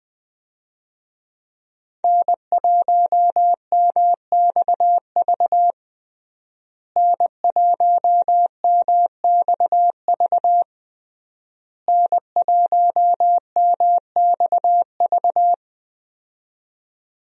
Morse code, heard as a keyed tone, about 20 wpm, 700 Hz. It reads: N1MXV N1MXV N1MXV
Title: USA Call Signs - 5 Characters - Repeated 3x - 20wpm